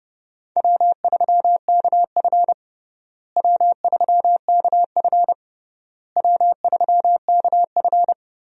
Morse code, heard as a keyed tone, about 32 wpm, 700 Hz. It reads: W3KF W3KF W3KF
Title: USA Call Signs - 4 Characters - Repeated 3x - 30wpm